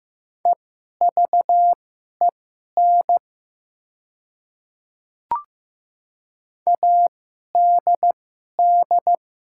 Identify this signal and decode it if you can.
Morse code, heard as a keyed tone, about 15 wpm, spaced as for 11 wpm, 700 Hz. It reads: EVEN ADD
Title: 2x Character Spacing - Top 300 Words - 15wpm